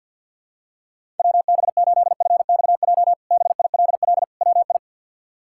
Morse code, heard as a keyed tone, about 48 wpm, 700 Hz. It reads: WB8FXJ BILL WI
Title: K1USN SST QSO Practice - 50wpm